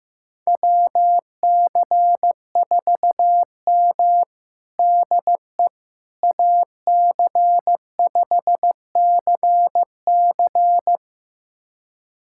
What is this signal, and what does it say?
Morse code, heard as a keyed tone, about 15 wpm, 700 Hz. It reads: WC4M DE AC5CC
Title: USA Callsign DE Callsign 15wpm